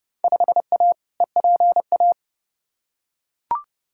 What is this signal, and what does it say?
Morse code, heard as a keyed tone, about 30 wpm, 700 Hz. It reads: 5A EPA
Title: Field Day Exchanges - 30wpm